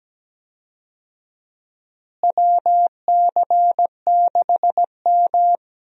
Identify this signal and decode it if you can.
Morse code, heard as a keyed tone, about 17 wpm, 700 Hz. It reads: WC6M
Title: USA Call Signs - 4 Characters - Repeated 2x - 17wpm